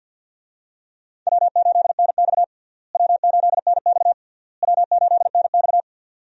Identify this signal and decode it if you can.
Morse code, heard as a keyed tone, about 52 wpm, 700 Hz. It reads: W8NX W8NX W8NX
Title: USA Call Signs - 4 Characters - Repeated 3x - 50wpm